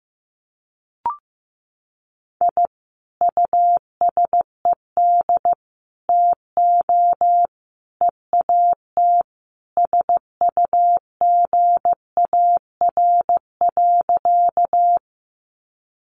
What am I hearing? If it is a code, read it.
Morse code, heard as a keyed tone, about 15 wpm, 700 Hz. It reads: I USED TO EAT SUGAR.
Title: Sentences from Top 1200 Words 15wpm